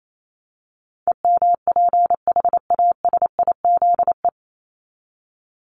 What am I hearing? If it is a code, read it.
Morse code, heard as a keyed tone, about 28 wpm, 700 Hz. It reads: EMPHASIZE
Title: Top 1251-1500 Words 28wpm